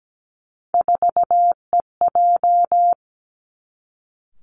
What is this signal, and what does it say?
Morse code, heard as a keyed tone, about 17 wpm, 700 Hz. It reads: 4EJ
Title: Three Challenging Letter-Number Combinations 17wpm